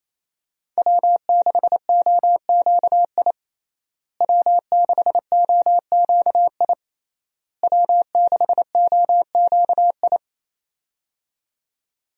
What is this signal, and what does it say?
Morse code, heard as a keyed tone, about 28 wpm, 700 Hz. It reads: W6OQI W6OQI W6OQI
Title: USA Call Signs - 5 Characters - Repeated 3x - 28wpm